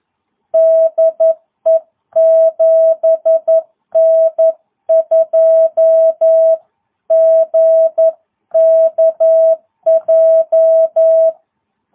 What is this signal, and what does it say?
Morse code, heard as a keyed tone, about 11 wpm, 660 Hz. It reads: DE 7N2GKJ